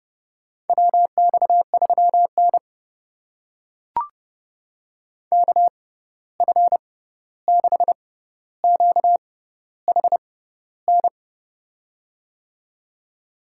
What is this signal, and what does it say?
Morse code, heard as a keyed tone, about 30 wpm, spaced as for 14 wpm, 700 Hz. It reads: WX3N KF6QHN
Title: USA Call Signs - 6x Character Spacing - 30wpm